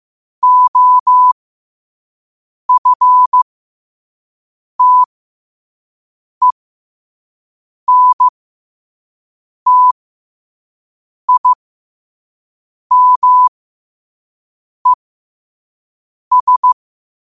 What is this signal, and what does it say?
Morse code, heard as a keyed tone, about 15 wpm, spaced as for 5 wpm, 1000 Hz. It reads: OFTENTIMES
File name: qotdfiles / 5wpm